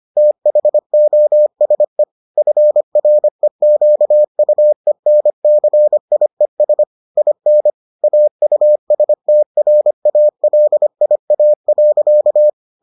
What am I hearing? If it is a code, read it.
Morse code, heard as a keyed tone, about 25 wpm, 600 Hz. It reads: THOSE FREQUENCIES IN AUSTRALIA.